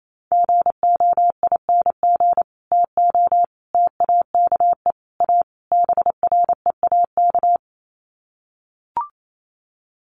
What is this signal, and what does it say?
Morse code, heard as a keyed tone, about 28 wpm, 700 Hz. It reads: GOING TO TAKE A BREAK
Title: Sets of 5 Words 28wpm